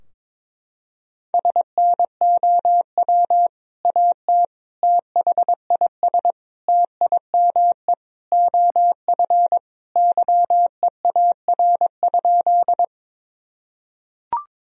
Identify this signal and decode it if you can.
Morse code, heard as a keyed tone, about 22 wpm, 700 Hz. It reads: SNOW AT THIS TIME OF YEAR?
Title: Sentences from Top 1000 Words - Review Part 3 - 22wpm